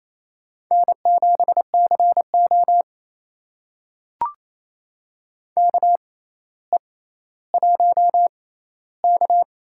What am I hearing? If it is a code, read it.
Morse code, heard as a keyed tone, about 29 wpm, spaced as for 15 wpm, 700 Hz. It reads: N7CO KE1K
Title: USA Call Signs - 4 Characters - 6x Character Spacing - 28wpm